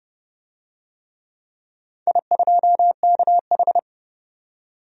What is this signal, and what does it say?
Morse code, heard as a keyed tone, about 30 wpm, 700 Hz. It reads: I2KH
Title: DXCC-100 Call Signs - 4 Characters - 30wpm